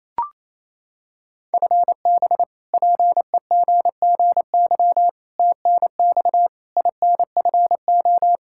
Morse code, heard as a keyed tone, about 27 wpm, 700 Hz. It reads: FB PEGGY TNX INFO
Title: Practice QSO - Exchange 3 - More Info - 28wpm